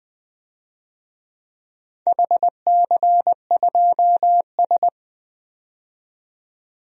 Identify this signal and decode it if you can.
Morse code, heard as a keyed tone, about 20 wpm, 700 Hz. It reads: HC2S
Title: DXCC-100 Call Signs - 4 Characters - 20wpm